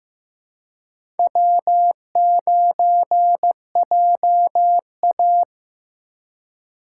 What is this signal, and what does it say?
Morse code, heard as a keyed tone, about 15 wpm, 700 Hz. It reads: W9JA
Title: USA Call Signs - 4x Character Spacing - 15wpm